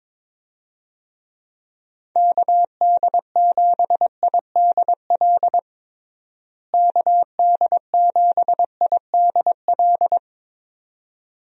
Morse code, heard as a keyed tone, about 22 wpm, 700 Hz. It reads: KD7IDL KD7IDL
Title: USA Call Signs - 6 Characters - Repeated 2x - 22wpm